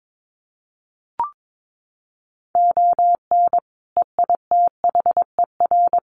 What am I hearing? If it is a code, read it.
Morse code, heard as a keyed tone, about 22 wpm, 700 Hz. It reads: ON EITHER
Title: Sets of 5 Words 22wpm